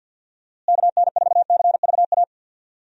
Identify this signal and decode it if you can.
Morse code, heard as a keyed tone, about 47 wpm, 700 Hz. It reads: KN4CVA
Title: USA Call Signs - 6 Characters - Repeated 2x - 50wpm